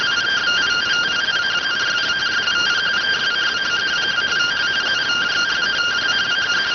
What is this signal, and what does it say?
Two-tone teleprinter signal, about 170 Hz shift.